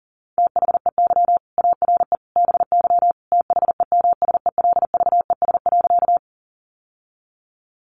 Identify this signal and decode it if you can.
Morse code, heard as a keyed tone, about 40 wpm, 690 Hz. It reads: THEY ARE BY THEMSELVES.
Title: Sentences from Top 800 Words 40wpm